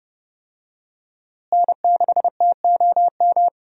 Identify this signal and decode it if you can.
Morse code, heard as a keyed tone, about 30 wpm, 700 Hz. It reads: N6TOM
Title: USA Call Signs - 5 Characters - Repeated 2x - 30wpm